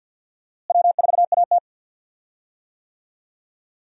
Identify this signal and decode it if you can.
Morse code, heard as a keyed tone, about 53 wpm, 700 Hz. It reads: W4AT
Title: USA Call Signs - 4 Characters - Repeated 2x - 50wpm